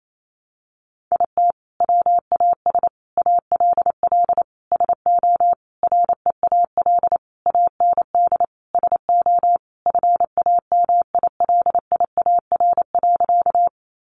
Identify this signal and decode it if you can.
Morse code, heard as a keyed tone, about 28 wpm, 700 Hz. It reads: IT WAS ALL SO REAL AND SO FAMILIAR.
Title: Sentences from Top 1000 Words 28wpm